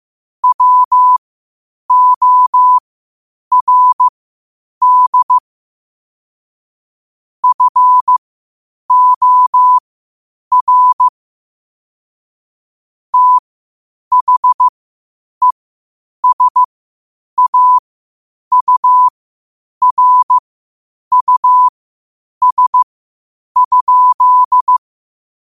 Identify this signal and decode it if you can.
Morse code, heard as a keyed tone, about 15 wpm, spaced as for 8 wpm, 1000 Hz. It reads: WORD FOR THESAURUS?
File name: qotdfiles / 8wpm